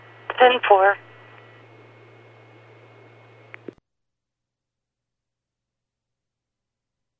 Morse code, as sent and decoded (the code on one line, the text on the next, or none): none